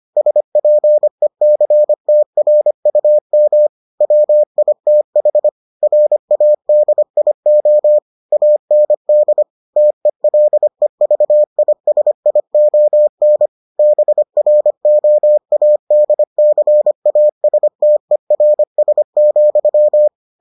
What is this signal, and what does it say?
Morse code, heard as a keyed tone, about 25 wpm, 600 Hz. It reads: SPECTRUM WITH RADIO AND TELEVISION BROADCASTERS,